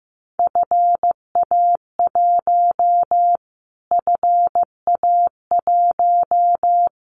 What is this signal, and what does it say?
Morse code, heard as a keyed tone, about 15 wpm, 700 Hz. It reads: FA1 FA1
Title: Callsign Prefixes for Top 25 DXCC - Repeated 3x - 15wpm